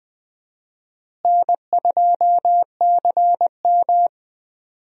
Morse code, heard as a keyed tone, about 20 wpm, 700 Hz. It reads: N2CM
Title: USA Call Signs - 4 Characters - Repeated 3x - 20wpm